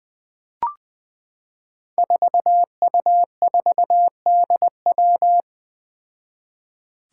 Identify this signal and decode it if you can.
Morse code, heard as a keyed tone, about 20 wpm, 700 Hz. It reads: 4U4DW
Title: Five Challenging Letter-Number Combinations 20wpm